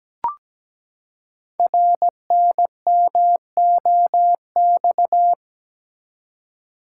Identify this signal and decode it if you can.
Morse code, heard as a keyed tone, about 17 wpm, 700 Hz. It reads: RNMOX
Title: Code Groups - 5 Letters - 17wpm